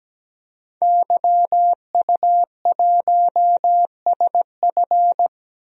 Morse code, heard as a keyed tone, about 17 wpm, 700 Hz. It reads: YU1SF
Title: DXCC-100 Call Signs - 5 Characters - Repeated 2x - 17wpm